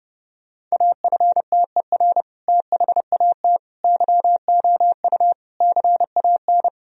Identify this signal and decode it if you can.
Morse code, heard as a keyed tone, about 29 wpm, 700 Hz. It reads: AFTER THAT YOU CAN
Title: Sentences from Top 1000 Words - Review Part 2 - 30wpm